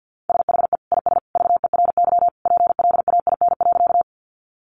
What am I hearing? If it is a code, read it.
Morse code, heard as a keyed tone, about 50 wpm, 700 Hz. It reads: SHE IS VERY PLAIN.
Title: Sentences from Top 400 Words 50wpm